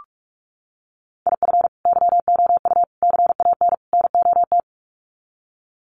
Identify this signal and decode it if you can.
Morse code, heard as a keyed tone, about 44 wpm, 700 Hz. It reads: IF YOU CAN NOT
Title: Sets of 4 Words 45wpm